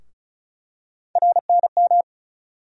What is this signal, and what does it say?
Morse code, heard as a keyed tone, about 34 wpm, 700 Hz. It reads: RNM
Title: USA Call Sign Postfixes - 3 Letters 35wpm